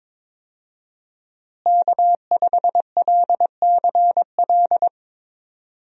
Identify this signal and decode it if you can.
Morse code, heard as a keyed tone, about 21 wpm, 700 Hz. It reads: K5LCL
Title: USA Call Signs - 5 Characters - Repeated 3x - 22wpm